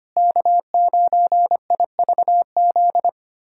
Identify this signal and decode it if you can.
Morse code, heard as a keyed tone, about 25 wpm, 700 Hz. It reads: K9IVZ
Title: USA Callsign DE Callsign - 5 Characters - 25wpm